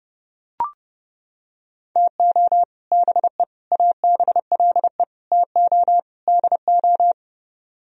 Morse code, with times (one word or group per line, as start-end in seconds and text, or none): 2.0-2.6 TO
2.9-3.4 BE
3.7-5.0 ABLE
5.3-6.0 TO
6.3-7.1 DO